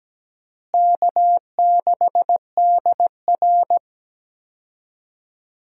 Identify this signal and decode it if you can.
Morse code, heard as a keyed tone, about 17 wpm, 700 Hz. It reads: K6DR
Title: USA Call Signs - 4 Characters - Repeated 2x - 17wpm